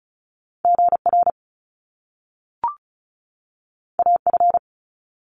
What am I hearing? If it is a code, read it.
Morse code, heard as a keyed tone, about 36 wpm, 700 Hz. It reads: GR AF
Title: Top Two Letter Combinations in Words 35wpm